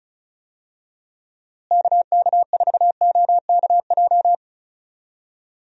Morse code, heard as a keyed tone, about 34 wpm, 700 Hz. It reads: KK4OKJ
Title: USA Call Signs - 6 Characters - Repeated 2x - 35wpm